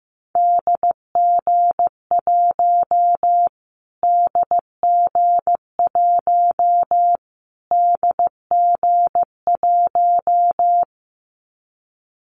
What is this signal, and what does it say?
Morse code, heard as a keyed tone, about 15 wpm, 700 Hz. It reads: DG1 DG1 DG1
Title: Callsign Prefixes for Top 25 DXCC - Repeated 3x - 15wpm